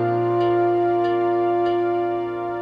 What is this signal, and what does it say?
There is no code.